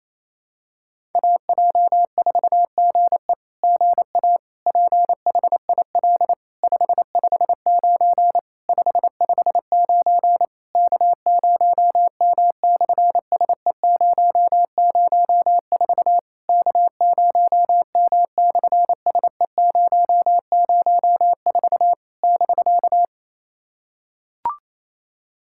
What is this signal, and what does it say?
Morse code, heard as a keyed tone, about 27 wpm, 700 Hz. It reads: AJ4GE GA PHIL 559 559 K0M/SE004 K0M/SE004 <BK>
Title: SOTA QSO Practice 28wpm